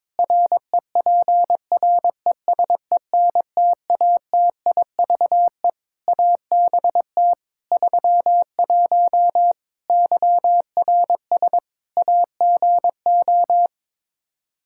Morse code, heard as a keyed tone, about 22 wpm, 700 Hz. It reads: REPRESENTATIVE ABT 31 YRS AGO
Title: Practice QSO - Exchange 3 - More Info - 22wpm